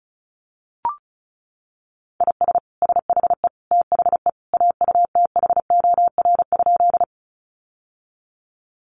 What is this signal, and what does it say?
Morse code, heard as a keyed tone, about 34 wpm, 700 Hz. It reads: IS SHE THE AUTHOR?
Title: Sentences from Top 1200 Words 35wpm